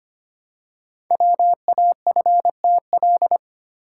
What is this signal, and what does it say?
Morse code, heard as a keyed tone, about 25 wpm, 700 Hz. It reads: WAFTL
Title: Code Groups - 5 Letters - 25wpm